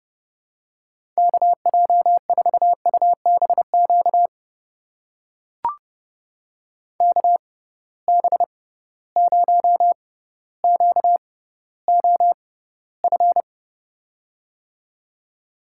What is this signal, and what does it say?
Morse code, heard as a keyed tone, about 29 wpm, spaced as for 15 wpm, 700 Hz. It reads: KJ4UBQ KB0QOF
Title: USA Call Signs - 6 Characters - 6x Character Spacing - 30wpm